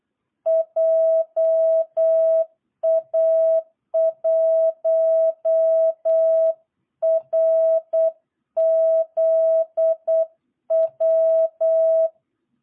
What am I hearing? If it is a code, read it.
Morse code, heard as a keyed tone, about 8 wpm, 660 Hz. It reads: JA1RZW
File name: jh8cbh / 2022cw